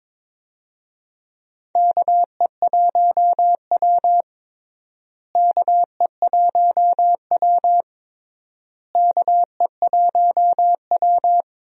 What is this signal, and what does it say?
Morse code, heard as a keyed tone, about 22 wpm, 700 Hz. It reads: KE1W KE1W KE1W
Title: USA Call Signs - 4 Characters - Repeated 3x - 22wpm